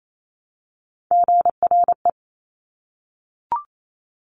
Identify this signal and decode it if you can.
Morse code, heard as a keyed tone, about 28 wpm, 700 Hz. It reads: GRE
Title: Top Three Letter Combinations in Words 28wpm